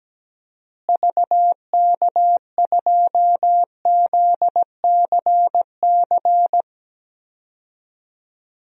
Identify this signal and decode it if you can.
Morse code, heard as a keyed tone, about 17 wpm, 700 Hz. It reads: VK2ZCC
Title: DXCC-100 Call Signs - 6 Characters - Repeated 2x - 17wpm